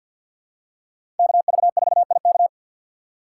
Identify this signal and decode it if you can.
Morse code, heard as a keyed tone, about 53 wpm, 700 Hz. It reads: KV4IK
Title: USA Call Signs - 5 Characters - 4x Character Spacing - 50wpm